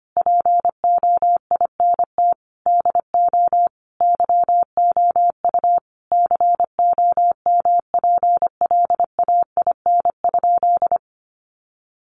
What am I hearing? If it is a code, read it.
Morse code, heard as a keyed tone, about 25 wpm, 700 Hz. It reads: POINT DO YOU COMPLAIN?
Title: Sentences from Top 1300 Words 25wpm